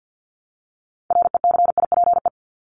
Morse code, heard as a keyed tone, about 50 wpm, 700 Hz. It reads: RECIPE
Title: Top 1001-1250 Words 50wpm